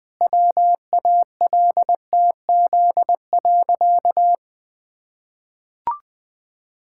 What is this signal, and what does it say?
Morse code, heard as a keyed tone, about 20 wpm, 700 Hz. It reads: WALTZ.
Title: Pangram Challenge v2 - 20wpm